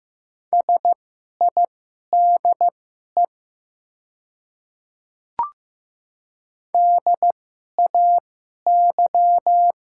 Morse code, heard as a keyed tone, about 15 wpm, spaced as for 11 wpm, 690 Hz. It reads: SIDE DAY
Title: 2x Character Spacing - Top 100 Words - 15wpm